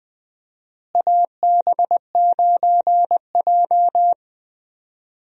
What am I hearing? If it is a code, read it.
Morse code, heard as a keyed tone, about 20 wpm, 700 Hz. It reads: AB9J
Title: USA Call Signs - 4 Characters - Repeated 2x - 20wpm